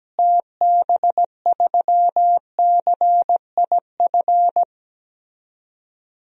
Morse code, heard as a keyed tone, about 17 wpm, 700 Hz. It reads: TB3CIF
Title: DXCC-100 Call Signs - 6 Characters - Repeated 2x - 17wpm